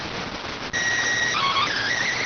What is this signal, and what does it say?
SSTV (slow-scan television), PD 290.